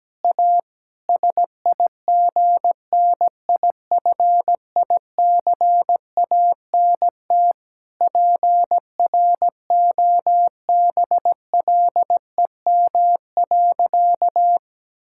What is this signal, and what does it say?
Morse code, heard as a keyed tone, about 17 wpm, 700 Hz. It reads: A SIGNIFICANT PROBLEM.
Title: Sentences from Top 1000 Words - Review Part 6 - 17wpm